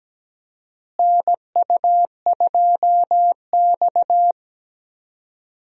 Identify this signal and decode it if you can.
Morse code, heard as a keyed tone, about 17 wpm, 700 Hz. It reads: NU2X
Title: USA Call Signs - 4 Characters - Repeated 3x - 17wpm